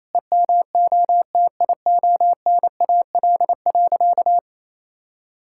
Morse code, heard as a keyed tone, about 28 wpm, 700 Hz. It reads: EMOTIONAL.
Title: Sentences from Top 1000 Words - Review Part 5 - 28wpm